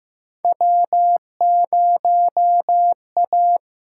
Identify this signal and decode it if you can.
Morse code, heard as a keyed tone, about 15 wpm, 700 Hz. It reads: W0A
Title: USA Call Signs - 8x Character Spacing - 15wpm